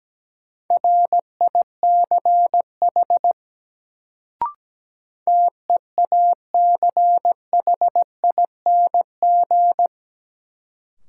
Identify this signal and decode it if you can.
Morse code, heard as a keyed tone, about 17 wpm, 700 Hz. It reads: RICH TEACHING
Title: Sentences from Top 1100 Words - New Word Drill Alt - 17wpm